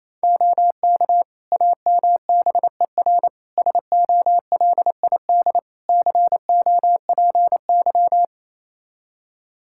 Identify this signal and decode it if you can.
Morse code, heard as a keyed tone, about 28 wpm, 700 Hz. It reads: OK AMBER SOLID COPY
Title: Practice QSO - Exchange 3 - More Info - 28wpm